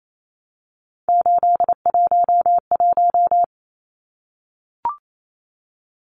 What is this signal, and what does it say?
Morse code, heard as a keyed tone, about 28 wpm, 700 Hz. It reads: 811
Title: Numbers - 3-Digit 28wpm